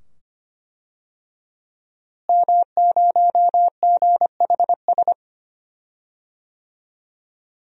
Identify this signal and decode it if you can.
Morse code, heard as a keyed tone, about 25 wpm, 700 Hz. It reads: M0GHS
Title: DXCC-100 Call Signs - 5 Characters - Repeated 2x - 25wpm